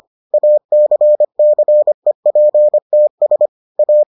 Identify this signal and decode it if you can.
Morse code, heard as a keyed tone, about 25 wpm, 600 Hz. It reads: ACCEPTS A